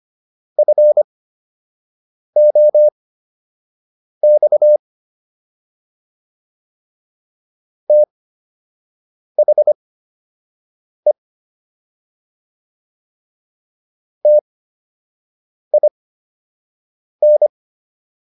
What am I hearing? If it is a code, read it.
Morse code, heard as a keyed tone, about 25 wpm, spaced as for 6 wpm, 600 Hz. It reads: FOX THE TIN